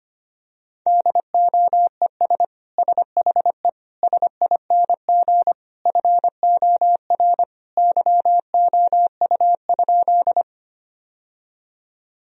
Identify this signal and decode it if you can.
Morse code, heard as a keyed tone, about 25 wpm, 700 Hz. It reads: DOES SHE SING FOR YOU?
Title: Sentences from Top 1000 Words - Review Part 2 - 25wpm